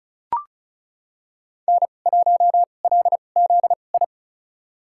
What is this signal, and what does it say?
Morse code, heard as a keyed tone, about 34 wpm, spaced as for 25 wpm, 700 Hz. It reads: N1LZI
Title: USA Call Signs - 5 Characters - 2x Character Spacing - 35wpm